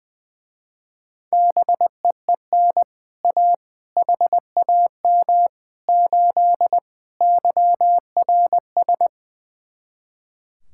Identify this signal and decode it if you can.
Morse code, heard as a keyed tone, about 20 wpm, 700 Hz. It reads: BEEN A HAM 8 YRS
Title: Speed-Racing Common QSO Phrases - 20wpm